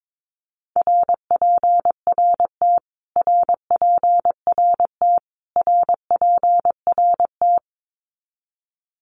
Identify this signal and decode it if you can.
Morse code, heard as a keyed tone, about 22 wpm, 700 Hz. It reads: RPRT RPRT RPRT
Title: Abbreviations Repeated 3x - 22wpm